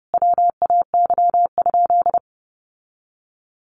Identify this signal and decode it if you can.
Morse code, heard as a keyed tone, about 30 wpm, 700 Hz. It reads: WAY?
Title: Sentences from Top 1100 Words 30wpm